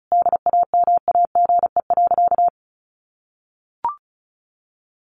Morse code, heard as a keyed tone, about 34 wpm, 700 Hz. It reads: DAMAGE.
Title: Sentences from Top 1100 Words 35wpm